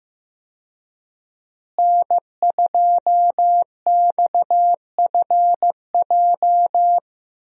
Morse code, heard as a keyed tone, about 15 wpm, 700 Hz. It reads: N2XFJ
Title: USA Call Signs - 5 Characters - Repeated 3x - 15wpm